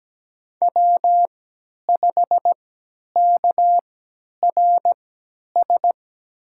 Morse code, heard as a keyed tone, about 17 wpm, spaced as for 10 wpm, 700 Hz. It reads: W5KRS
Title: USA Call Signs - 5 Characters - 3x Character Spacing - 17wpm